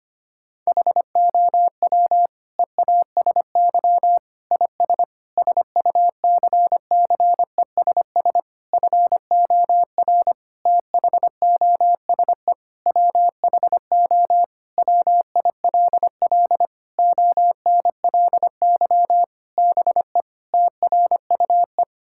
Morse code, heard as a keyed tone, about 25 wpm, 700 Hz. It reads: HOW EASY IS SUCCESS FOR THOSE WHO WILL ONLY BE TRUE
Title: Sentences from Top 1000 Words - Review Part 6 - 25wpm